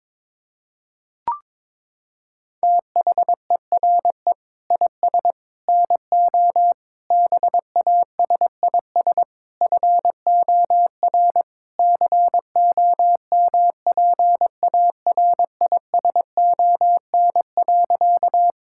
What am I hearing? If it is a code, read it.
Morse code, heard as a keyed tone, about 22 wpm, 700 Hz. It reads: THERE IS NO BASIS FOR COMPARISON.
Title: Sentences from Top 1600 Words 22wpm